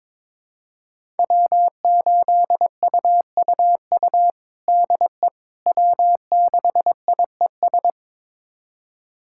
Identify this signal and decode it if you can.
Morse code, heard as a keyed tone, about 22 wpm, 700 Hz. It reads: W8UUU DE W6IES
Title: USA Callsign DE Callsign - 5 Characters - 22wpm